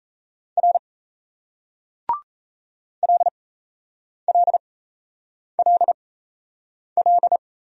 Morse code, heard as a keyed tone, about 37 wpm, spaced as for 8 wpm, 700 Hz. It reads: R LLLL